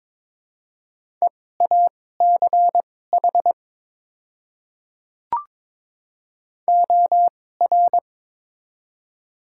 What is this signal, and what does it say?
Morse code, heard as a keyed tone, about 22 wpm, spaced as for 16 wpm, 700 Hz. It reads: EACH OR
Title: 2x Character Spacing - Top 200 Words - 22wpm